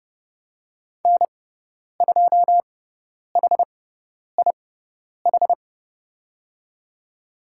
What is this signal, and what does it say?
Morse code, heard as a keyed tone, about 30 wpm, spaced as for 10 wpm, 700 Hz. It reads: N2HIH
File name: Lesson 17g - Letter u - Call Signs 30wpm